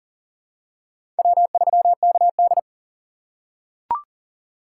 Morse code, heard as a keyed tone, about 40 wpm, 700 Hz. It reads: W3KD
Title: USA Call Signs - 4 Characters - 3x Character Spacing - 40wpm